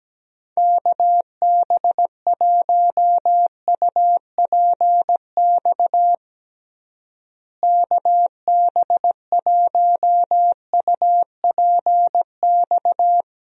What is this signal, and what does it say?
Morse code, heard as a keyed tone, about 17 wpm, 700 Hz. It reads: KB1UPX KB1UPX
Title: USA Call Signs - 6 Characters - Repeated 3x - 17wpm